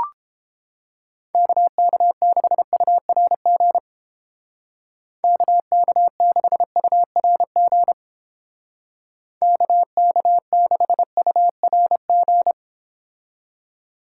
Morse code, heard as a keyed tone, about 29 wpm, 700 Hz. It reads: KK6URG KK6URG KK6URG